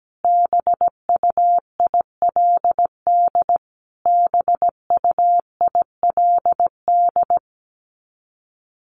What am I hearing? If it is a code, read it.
Morse code, heard as a keyed tone, about 17 wpm, 700 Hz. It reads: BUILD BUILD
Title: Top 300 Words Repeated 17wpm